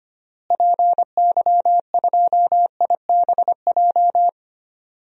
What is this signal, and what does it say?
Morse code, heard as a keyed tone, about 25 wpm, 700 Hz. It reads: PY2IBJ
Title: DXCC-100 Call Signs - 6 Characters - Repeated 3x - 25wpm